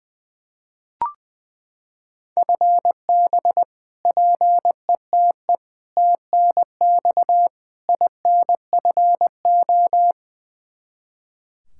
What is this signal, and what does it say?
Morse code, heard as a keyed tone, about 20 wpm, 700 Hz. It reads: FB PETE TNX INFO
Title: Practice QSO - Exchange 3 - More Info - 20wpm